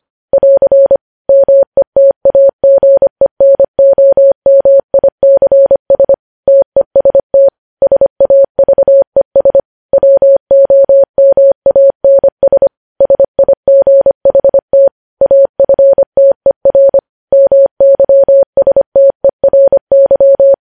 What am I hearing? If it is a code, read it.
Morse code, heard as a keyed tone, about 25 wpm, 550 Hz. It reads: <AR> METAGENOMICS TEST SAVES WOMANS SIGHT AFTER MYSTERY
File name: podcast / News headlines-1x-25x25